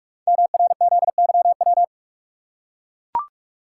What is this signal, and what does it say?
Morse code, heard as a keyed tone, about 44 wpm, 700 Hz. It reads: MRZYW
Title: Code Groups - 5 Letters - 45wpm